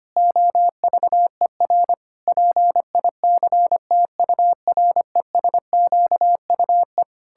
Morse code, heard as a keyed tone, about 25 wpm, 700 Hz. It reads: OVER PICTURESQUE
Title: Pangram Challenge v2 - 25wpm